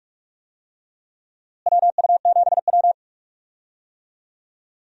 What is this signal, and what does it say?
Morse code, heard as a keyed tone, about 44 wpm, 700 Hz. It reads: WU7W
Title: USA Call Signs - 4 Characters - Repeated 3x - 45wpm